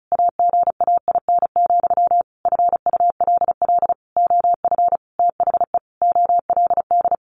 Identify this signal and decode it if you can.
Morse code, heard as a keyed tone, about 34 wpm, 700 Hz. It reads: AGAIN, FULL OF THE OLD
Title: A Sound Of Thunder 35wpm